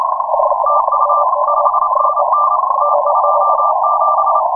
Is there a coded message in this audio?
Yes: Morse code from 0.7 to 4.4 s, heard as a keyed tone, about 28 wpm, 1250 Hz, with noise. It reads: TU 5NN T27